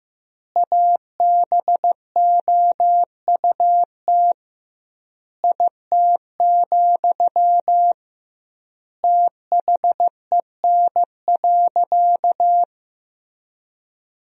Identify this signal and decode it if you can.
Morse code, heard as a keyed tone, about 15 wpm, 700 Hz. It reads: ABOUT IT, THEN.
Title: 2x Word-Spacing - Sentences from Top 200 Words - 15wpm